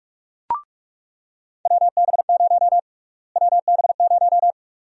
Morse code, heard as a keyed tone, about 47 wpm, 700 Hz. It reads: WB0 WB0
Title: USA Call Sign Prefixes - Repeated 2x - 45wpm